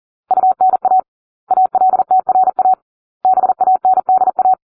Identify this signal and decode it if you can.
Morse code, heard as a keyed tone, about 40 wpm, 750 Hz. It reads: UNA ALTRA BANDA